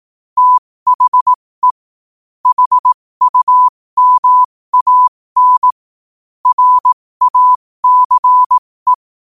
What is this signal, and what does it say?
Morse code, heard as a keyed tone, about 18 wpm, spaced as for 15 wpm, 1000 Hz. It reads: THE HUMAN RACE
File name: qotdfiles / 15wpm